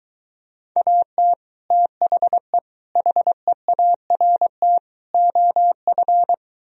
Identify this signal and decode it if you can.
Morse code, heard as a keyed tone, about 23 wpm, 700 Hz. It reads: AT THE HEART OF